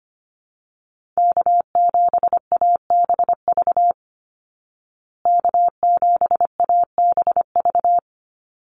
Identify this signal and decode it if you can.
Morse code, heard as a keyed tone, about 25 wpm, 700 Hz. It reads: K7ABV K7ABV
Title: Call Signs Repeated 3x - 25wpm